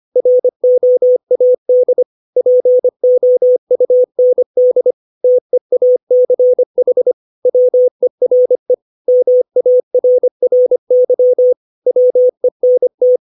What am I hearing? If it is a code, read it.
Morse code, heard as a keyed tone, about 25 wpm, 500 Hz. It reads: ROAD POUND TEACH WERE MARRY WENT
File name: 632 words / 632 Words 25WPM